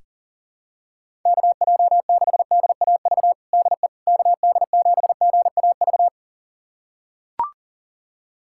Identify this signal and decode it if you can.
Morse code, heard as a keyed tone, about 40 wpm, 700 Hz. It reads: KJ6DAV DE KD7GAV
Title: USA Callsign DE Callsign - 6 Characters - 40wpm